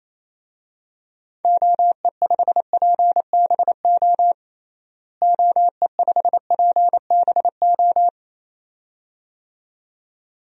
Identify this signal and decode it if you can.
Morse code, heard as a keyed tone, about 28 wpm, 700 Hz. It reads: OE5PBO OE5PBO
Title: DXCC-100 Call Signs - 6 Characters - Repeated 2x - 28wpm